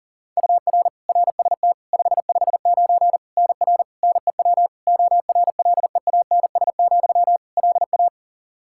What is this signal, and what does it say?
Morse code, heard as a keyed tone, about 40 wpm, 700 Hz. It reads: UR RST 559 NR NEW ORLEANS, LA
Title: K1USN SST QSO Practice 40wpm